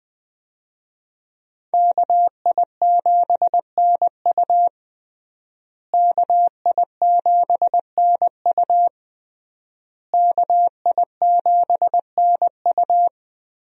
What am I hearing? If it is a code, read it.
Morse code, heard as a keyed tone, about 20 wpm, 700 Hz. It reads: KI7NU KI7NU KI7NU
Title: USA Call Signs - 5 Characters - Repeated 3x - 20wpm